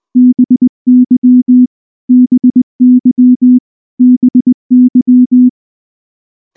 Morse code, heard as a keyed tone, about 19 wpm, 250 Hz. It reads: BY BY BY
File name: morse / morse-training-20WPM-2